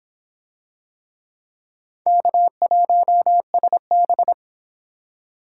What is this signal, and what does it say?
Morse code, heard as a keyed tone, about 24 wpm, 700 Hz. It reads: K1SB